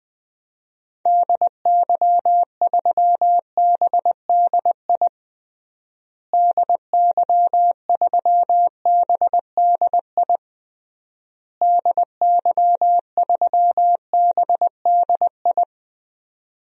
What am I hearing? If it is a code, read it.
Morse code, heard as a keyed tone, about 20 wpm, 700 Hz. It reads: DY3BDI DY3BDI DY3BDI
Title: DXCC-100 Call Signs - 6 Characters - Repeated 3x - 20wpm